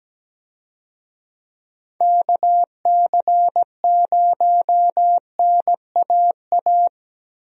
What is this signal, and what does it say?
Morse code, heard as a keyed tone, about 17 wpm, 700 Hz. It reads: KC0NAA
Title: USA Call Signs - 6 Characters - Repeated 3x - 17wpm